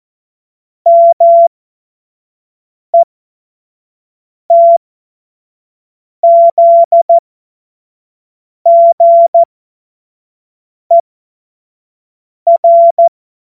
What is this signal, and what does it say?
Morse code, heard as a keyed tone, about 14 wpm, spaced as for 5 wpm, 680 Hz. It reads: METZGER
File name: Morsekurs / Witze / Witz 12. Oktober 2021, 25 F70